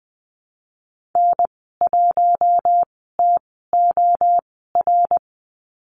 Morse code, heard as a keyed tone, about 20 wpm, spaced as for 15 wpm, 700 Hz. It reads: N1TOR
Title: 055 Lesson 10g - Letter r - Call Signs 20wpm